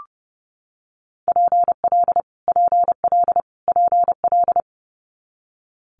Morse code, heard as a keyed tone, about 30 wpm, 690 Hz. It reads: PL PL PL
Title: Multi-letter Phonemes Repeated 3x 30wpm